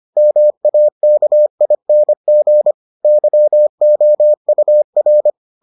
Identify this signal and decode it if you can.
Morse code, heard as a keyed tone, about 25 wpm, 600 Hz. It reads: MAKING YOUR